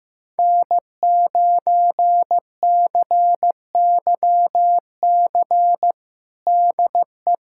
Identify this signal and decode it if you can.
Morse code, heard as a keyed tone, about 15 wpm, 700 Hz. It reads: N9CYC DE
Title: USA Callsign DE Callsign - 5 Characters - 15wpm